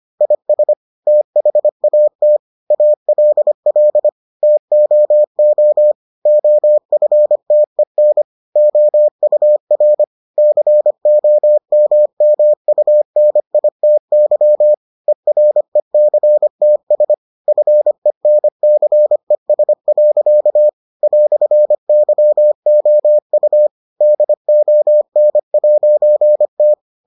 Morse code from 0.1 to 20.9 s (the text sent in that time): IS THAT ALL TOO OFTEN OUR COMMUNITY ERECTS FENCES.